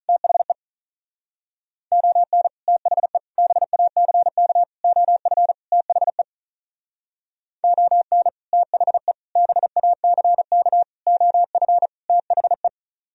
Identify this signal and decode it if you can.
Morse code, heard as a keyed tone, about 43 wpm, 700 Hz. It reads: THE ON THE BACK OF THE ON THE BACK OF THE
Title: Speed-Racing - Sets of 5 Words 35wpm